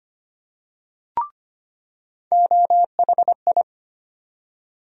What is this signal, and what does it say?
Morse code, heard as a keyed tone, about 25 wpm, 700 Hz. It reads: OHI
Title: USA Call Sign Postfixes - 3 Letters 25wpm